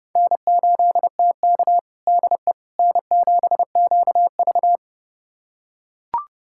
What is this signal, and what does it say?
Morse code, heard as a keyed tone, about 32 wpm, 700 Hz. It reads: N8TK DE N7QV
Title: USA Callsign DE Callsign - 4 Characters - 30wpm